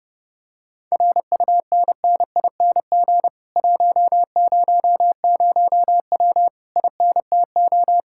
Morse code, heard as a keyed tone, about 28 wpm, 700 Hz. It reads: RUNNING 100W INTO